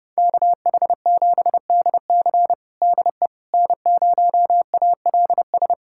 Morse code, heard as a keyed tone, about 32 wpm, 700 Hz. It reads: KH7DC DE N0ALS
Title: USA Callsign DE Callsign - 5 Characters - 30wpm